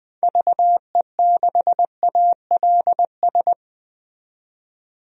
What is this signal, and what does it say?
Morse code, heard as a keyed tone, about 20 wpm, 700 Hz. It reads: VE6ALS
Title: DXCC-100 Call Signs - 6 Characters - 20wpm